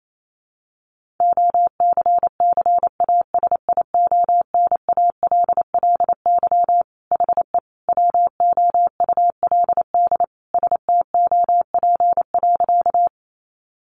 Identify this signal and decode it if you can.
Morse code, heard as a keyed tone, about 28 wpm, 700 Hz. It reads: OCCASIONALLY HE WOULD STOP.
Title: Sentences from Top 1300 Words 28wpm